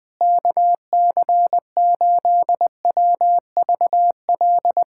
Morse code, heard as a keyed tone, about 20 wpm, 700 Hz. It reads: KC8WVL
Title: USA Callsign DE Callsign 20wpm